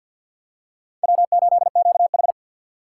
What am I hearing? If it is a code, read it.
Morse code, heard as a keyed tone, about 51 wpm, 700 Hz. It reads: W8QH
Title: USA Call Signs - 4 Characters - 2x Character Spacing - 50wpm